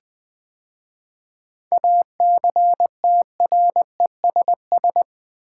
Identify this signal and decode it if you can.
Morse code, heard as a keyed tone, about 20 wpm, 700 Hz. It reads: ACTRESS
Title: 7 Letter Words - First 3 Letters Given - 20wpm